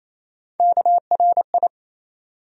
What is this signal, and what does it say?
Morse code, heard as a keyed tone, about 28 wpm, 700 Hz. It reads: KRI
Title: USA Call Sign Postfixes - 3 Letters 28wpm